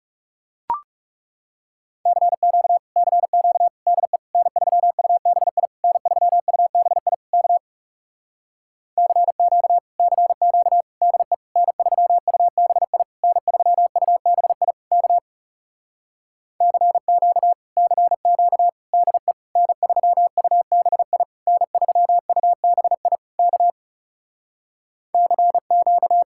Morse code, heard as a keyed tone, about 40 wpm, 700 Hz. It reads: CQ CQ DE N3UBI N3UBI K CQ CQ DE N3UBI N3UBI K CQ CQ DE N3UBI N3UBI K CQ